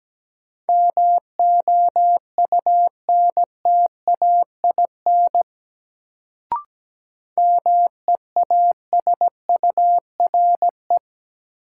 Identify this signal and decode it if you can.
Morse code, heard as a keyed tone, about 17 wpm, 700 Hz. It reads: MOUNTAIN MEASURE
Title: Sentences from Top 1000 Words - Review Part 1 - New Word Drill - 17wpm